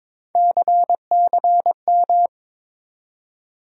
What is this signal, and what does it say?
Morse code, heard as a keyed tone, about 22 wpm, 700 Hz. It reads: CCM
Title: USA Call Sign Postfixes - 3 Letters 22wpm